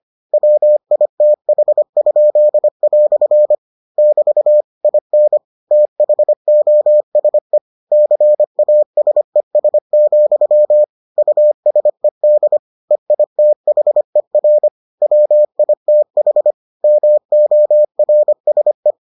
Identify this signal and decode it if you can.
Morse code, heard as a keyed tone, about 25 wpm, 600 Hz. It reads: WITH?" <BT> IN THOSE CASES, USED EITHER WITH MORSE